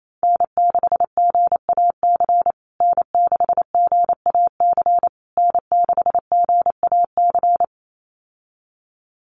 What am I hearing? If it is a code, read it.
Morse code, heard as a keyed tone, about 28 wpm, 700 Hz. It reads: N6GAC N6GAC N6GAC
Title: USA Call Signs Repeated 3x 28wpm